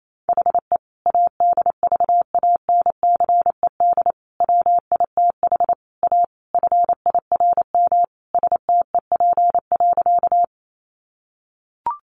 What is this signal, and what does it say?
Morse code, heard as a keyed tone, about 28 wpm, 700 Hz. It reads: HE ADVANCED WITH A FIRM STEP.
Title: Sentences from Top 1400 Words 28wpm